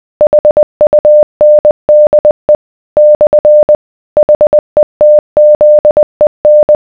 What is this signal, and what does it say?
Morse code, heard as a keyed tone, about 20 wpm, 600 Hz. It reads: HUNDE / HETZEN